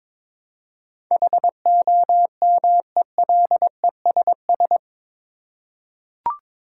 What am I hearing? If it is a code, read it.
Morse code, heard as a keyed tone, about 22 wpm, 700 Hz. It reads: HOMELESS
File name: practice / 22wpm / Eight Letter Words - 22wpm